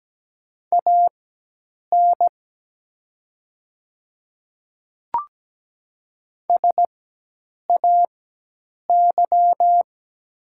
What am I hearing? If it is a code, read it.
Morse code, heard as a keyed tone, about 17 wpm, spaced as for 8 wpm, 700 Hz. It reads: AN SAY
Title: 4x Character Spacing - Top 200 Words - 17wpm